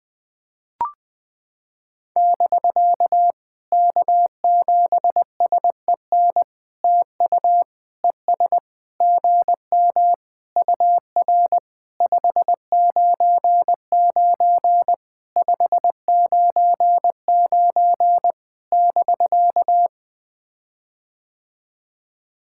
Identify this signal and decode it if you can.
Morse code, heard as a keyed tone, about 20 wpm, 700 Hz. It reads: <BK> K7SEN TU ES GM UR 599 599 <BK>
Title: POTA QSO Practice - 20wpm